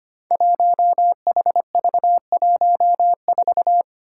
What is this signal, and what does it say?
Morse code, heard as a keyed tone, about 25 wpm, 700 Hz. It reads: 1HV14
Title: Five Challenging Letter-Number Combinations 25wpm